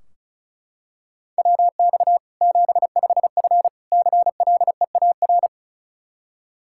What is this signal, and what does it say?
Morse code, heard as a keyed tone, about 34 wpm, 700 Hz. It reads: WX 75F CLEAR
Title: K1USN SST QSO Practice 35wpm